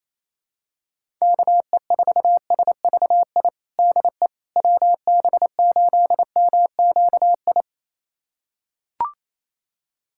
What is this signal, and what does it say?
Morse code, heard as a keyed tone, about 27 wpm, 700 Hz. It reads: KE4SVI DE WB8MQI
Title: USA Callsign DE Callsign - 6 Characters - 28wpm